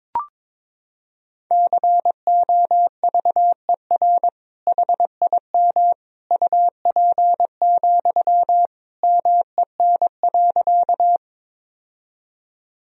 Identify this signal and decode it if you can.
Morse code, heard as a keyed tone, about 22 wpm, 700 Hz. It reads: COVER HIM UP, MEN.
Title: Sentences from Top 1000 Words - Review Part 1 - 22wpm